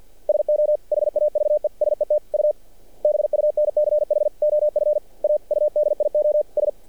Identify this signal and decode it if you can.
Morse code, heard as a keyed tone, about 54 wpm, 600 Hz. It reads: SO HAVE SET U BUNCH OF AUDIOS
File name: archived / CRSnet